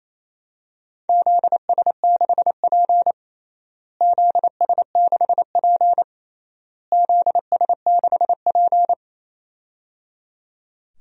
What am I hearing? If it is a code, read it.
Morse code, heard as a keyed tone, about 28 wpm, 700 Hz. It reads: ZS6P ZS6P ZS6P
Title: DXCC-100 Call Signs - 4 Characters - Repeated 3x - 28wpm